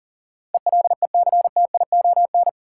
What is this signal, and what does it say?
Morse code, heard as a keyed tone, about 40 wpm, 700 Hz. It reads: ELECTION
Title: Sentences from Top 1500 Words - New Word Drill - 40wpm